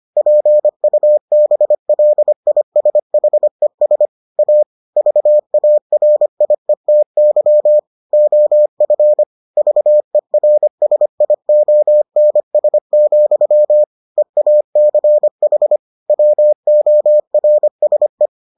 Morse code, heard as a keyed tone, about 25 wpm, 600 Hz. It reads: PUBLISHES A VARIETY OF VERSIONS, EACH WORSE